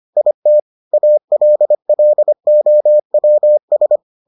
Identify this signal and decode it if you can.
Morse code, heard as a keyed tone, about 25 wpm, 600 Hz. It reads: IT ALLOWS